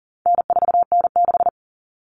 Morse code, heard as a keyed tone, about 40 wpm, 700 Hz. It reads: N4N6
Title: Common Cut Numbers - 4-Digit 40wpm